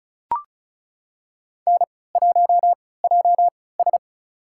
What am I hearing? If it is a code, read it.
Morse code, heard as a keyed tone, about 36 wpm, spaced as for 20 wpm, 690 Hz. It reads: N1JS
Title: USA Call Signs - 3x Character Spacing - 35wpm